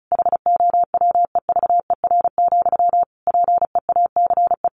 Morse code, heard as a keyed tone, about 34 wpm, 700 Hz. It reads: HOWEVER, PEACE
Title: Sentences from Top 1500 Words 35wpm